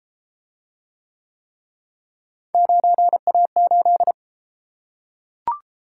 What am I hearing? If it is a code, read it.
Morse code, heard as a keyed tone, about 31 wpm, 700 Hz. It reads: 9A8